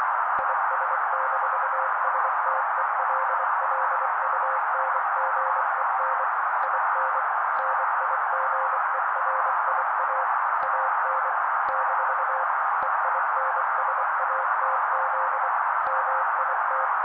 Morse code, heard as a keyed tone, about 24 wpm, 530 Hz. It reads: IS-STELLUNGEN IN NIGERIA AN - EINSATZ MIT
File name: daily / 23wpm noise